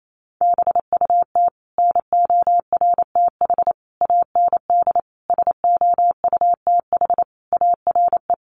Morse code, heard as a keyed tone, about 28 wpm, 700 Hz. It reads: BUT NORTH AND SOUTH ARE
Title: Sentences from Top 1100 Words 28wpm